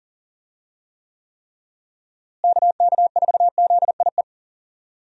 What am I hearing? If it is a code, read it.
Morse code, heard as a keyed tone, about 40 wpm, 700 Hz. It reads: KK4ZIE
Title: USA Call Signs - 6 Characters - Repeated 3x - 40wpm